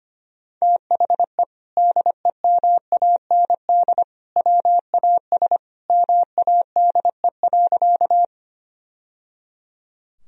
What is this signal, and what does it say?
Morse code, heard as a keyed tone, about 25 wpm, 700 Hz. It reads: THE DEMAND WAS MADE.
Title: Sentences from Top 1000 Words - Review Part 4 - 25wpm